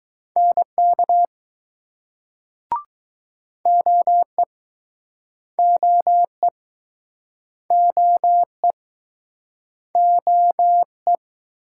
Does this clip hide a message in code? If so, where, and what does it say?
Yes: Morse code from 0.4 to 11.2 s, heard as a keyed tone, about 20 wpm, 700 Hz. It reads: NK OE OE OE OE